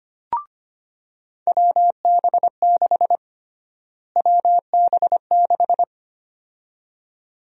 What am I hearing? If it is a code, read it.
Morse code, heard as a keyed tone, about 25 wpm, 700 Hz. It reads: WB6 WB6
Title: USA Call Sign Prefixes - Repeated 2x - 25wpm